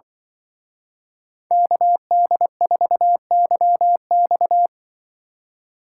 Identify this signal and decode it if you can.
Morse code, heard as a keyed tone, about 23 wpm, 700 Hz. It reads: KD4YX
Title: Speed Racing - USA 2x2 Call Signs 20wpm